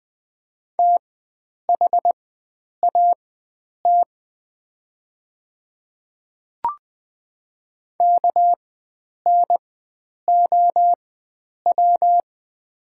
Morse code, heard as a keyed tone, about 20 wpm, spaced as for 9 wpm, 700 Hz. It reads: THAT KNOW
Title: 4x Character Spacing - Top 100 Words - 20wpm